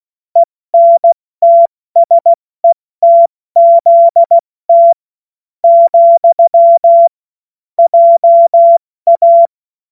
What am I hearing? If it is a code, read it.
Morse code, heard as a keyed tone, about 16 wpm, 680 Hz. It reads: ENTSETZT , JA
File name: Morsekurs / Witze / Witz 11. Juli 2023 70 F80